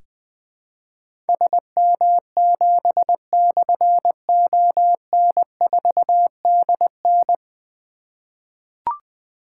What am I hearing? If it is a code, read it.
Morse code, heard as a keyed tone, about 20 wpm, 700 Hz. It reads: SM7/ON4DN
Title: Difficult Call Signs 20wpm